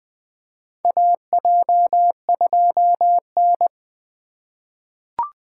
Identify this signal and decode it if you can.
Morse code, heard as a keyed tone, about 20 wpm, 700 Hz. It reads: AJ2N
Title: USA Call Signs - 4 Characters - 2x Character Spacing - 20wpm